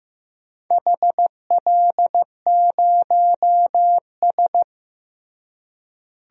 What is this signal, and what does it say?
Morse code, heard as a keyed tone, about 16 wpm, 700 Hz. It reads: HL0S
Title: DXCC-100 Call Signs - 4 Characters - Repeated 3x - 15wpm